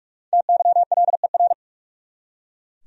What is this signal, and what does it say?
Morse code, heard as a keyed tone, about 44 wpm, 700 Hz. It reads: TYLER
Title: USA Common Names - Max 5 Letters - 45wpm